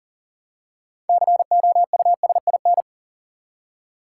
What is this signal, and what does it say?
Morse code, heard as a keyed tone, about 40 wpm, 700 Hz. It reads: COUSIN
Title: Six Letter Words - 40wpm